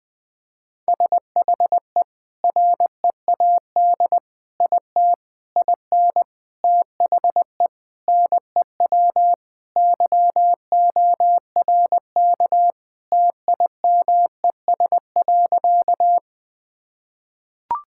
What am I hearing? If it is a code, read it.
Morse code, heard as a keyed tone, about 20 wpm, 700 Hz. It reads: SHE READ IT IN THE NEW YORK TIMES.
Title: Sentences with 5-Word Phrases - 20wpm